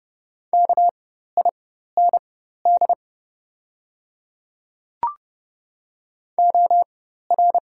Morse code, heard as a keyed tone, about 30 wpm, spaced as for 14 wpm, 690 Hz. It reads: KIND OR
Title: 4x Character Spacing - Top 200 Words - 30wpm